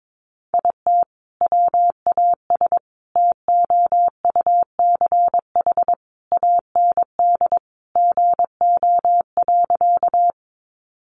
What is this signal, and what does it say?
Morse code, heard as a keyed tone, about 22 wpm, 700 Hz. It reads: IT WAS TOUCH AND GO.
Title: Sentences from Top 1200 Words 22wpm